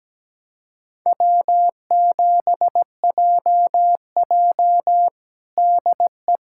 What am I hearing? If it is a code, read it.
Morse code, heard as a keyed tone, about 17 wpm, 700 Hz. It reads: W7JJ DE
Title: USA Callsign DE Callsign - 4 Characters - 17wpm